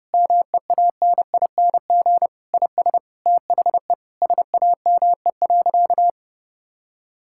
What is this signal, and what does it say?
Morse code, heard as a keyed tone, about 32 wpm, 700 Hz. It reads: MEANING IS THE SAME.
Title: Sentences from Top 1000 Words - Review Part 6 - 30wpm